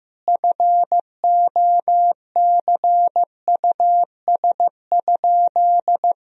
Morse code, heard as a keyed tone, about 15 wpm, 700 Hz. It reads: FOCUS?
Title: Sentences from Top 1000 Words - Review Part 4 - 15wpm